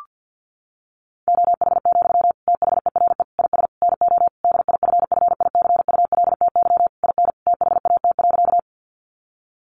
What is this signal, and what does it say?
Morse code, heard as a keyed tone, about 50 wpm, 700 Hz. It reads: OH, THERE IS NO DIFFICULTY IN THAT.
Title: Sentences from Top 1500 Words 50wpm